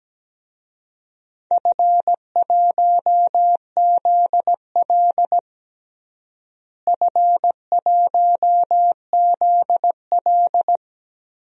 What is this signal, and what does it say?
Morse code, heard as a keyed tone, about 17 wpm, 700 Hz. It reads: F1ZL F1ZL
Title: DXCC-100 Call Signs - 4 Characters - Repeated 2x - 17wpm